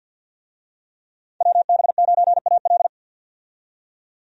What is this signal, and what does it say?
Morse code, heard as a keyed tone, about 47 wpm, 700 Hz. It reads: WB9AL
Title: USA Call Signs - 5 Characters - 50wpm